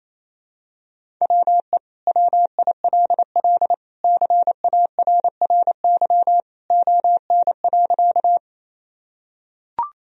Morse code, heard as a keyed tone, about 28 wpm, 700 Hz. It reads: WE WILL CARRY ON.
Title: Sentences from Top 1000 Words - Review Part 1 - 28wpm